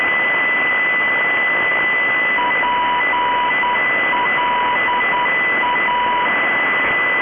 Morse code, heard as a keyed tone, about 10 wpm, 980 Hz. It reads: PLA